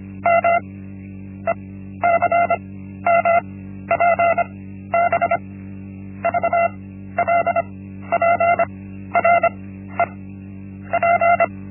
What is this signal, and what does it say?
Morse code, heard as a keyed tone, about 25 wpm, 680 Hz. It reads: M E C M P B V L P R E P
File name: Linksmakalnis-1